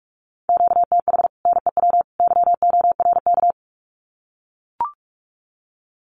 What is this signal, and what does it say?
Morse code, heard as a keyed tone, about 44 wpm, 700 Hz. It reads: QTH NEW YORK
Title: Common QSO Phrases 45wpm